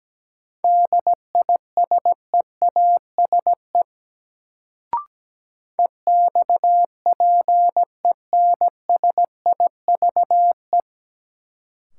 Morse code, heard as a keyed tone, about 17 wpm, 700 Hz. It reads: DISEASE EXPENSIVE
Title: Sentences from Top 1000 Words - Review Part 6 - New Word Drill - 17wpm